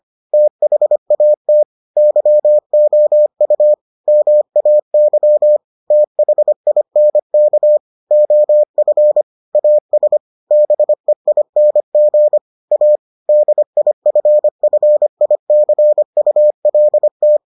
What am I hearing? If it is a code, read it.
Morse code, heard as a keyed tone, about 25 wpm, 600 Hz. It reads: THAT YOU MAY THINK OF AS BEING A DIFFICULT